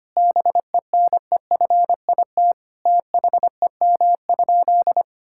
Morse code, heard as a keyed tone, about 25 wpm, 700 Hz. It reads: BENEFIT THEM?
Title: Sentences from Top 1000 Words - Review Part 6 - 25wpm